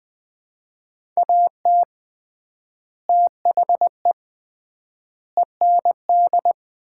Morse code, heard as a keyed tone, about 21 wpm, 690 Hz. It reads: AT THE END
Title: 3x Word-Spacing - Sets of 5 Words - 20wpm